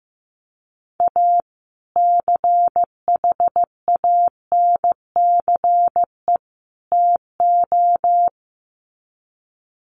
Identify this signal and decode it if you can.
Morse code, heard as a keyed tone, about 15 wpm, 700 Hz. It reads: A CHANCE TO
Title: Sets of 3 Words 15wpm